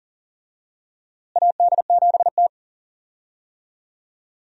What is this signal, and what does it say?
Morse code, heard as a keyed tone, about 40 wpm, 700 Hz. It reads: AD7T
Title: USA Call Signs - 4 Characters - Repeated 2x - 40wpm